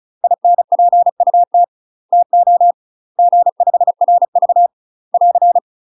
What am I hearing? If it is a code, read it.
Morse code, heard as a keyed tone, about 34 wpm, 700 Hz. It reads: INPUT TO G5RV <AR>
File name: UK MASTER QSOs TRAINING / 35WPM / UK MASTER QSOs 0040